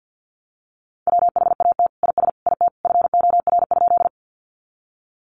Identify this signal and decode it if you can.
Morse code, heard as a keyed tone, about 50 wpm, 700 Hz. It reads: WHAT IS IT FOR?
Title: Sentences from Top 100 Words 50wpm